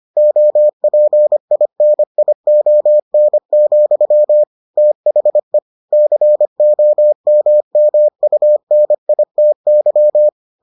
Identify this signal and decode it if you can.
Morse code, heard as a keyed tone, about 25 wpm, 600 Hz. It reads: OPINION, THE COMMUNITY